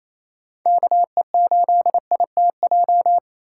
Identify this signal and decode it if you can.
Morse code, heard as a keyed tone, about 32 wpm, 700 Hz. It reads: KE8ITJ
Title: Speed Racing - USA 2x3 Call Signs 28wpm